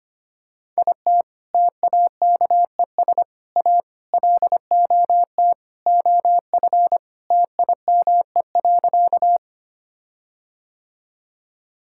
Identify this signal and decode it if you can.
Morse code, heard as a keyed tone, about 25 wpm, 700 Hz. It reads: IT TAKES A LOT OF TIME.
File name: practice / 25wpm / Sentences with 4-Word Phrases - 25wpm